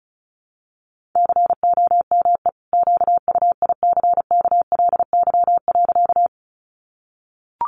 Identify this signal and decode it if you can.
Morse code, heard as a keyed tone, about 34 wpm, 700 Hz. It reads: COME QUICKLY.
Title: Sentences from Top 700 Words 35wpm